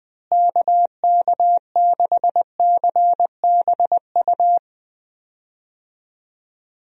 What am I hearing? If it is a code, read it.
Morse code, heard as a keyed tone, about 21 wpm, 700 Hz. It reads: KK6CBU
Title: USA Call Signs - 6 Characters - Repeated 3x - 20wpm